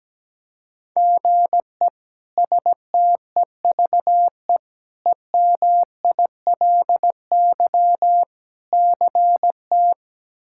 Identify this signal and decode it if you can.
Morse code, heard as a keyed tone, about 17 wpm, 700 Hz. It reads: GE STEVE EMILY CT
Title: K1USN SST QSO Practice - 17wpm